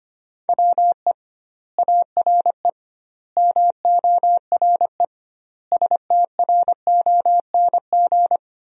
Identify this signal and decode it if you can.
Morse code, heard as a keyed tone, about 25 wpm, 700 Hz. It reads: WE ARE MORE STRONG
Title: 2x Word-Spacing - Sentences from Top 500 Words - 25wpm